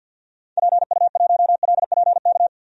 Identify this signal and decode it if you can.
Morse code, heard as a keyed tone, about 51 wpm, 700 Hz. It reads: PU1LPK
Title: DXCC-100 Call Signs - 6 Characters - 50wpm